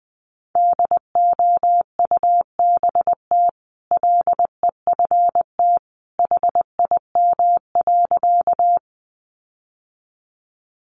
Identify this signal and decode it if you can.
Morse code, heard as a keyed tone, about 20 wpm, 700 Hz. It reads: DOUBT LEFT HIM.
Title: Sentences from Top 1500 Words 20wpm